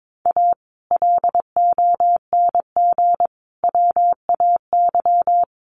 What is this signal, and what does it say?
Morse code, heard as a keyed tone, about 22 wpm, 700 Hz. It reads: A LONG WAY
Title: Sets of 5 Words 22wpm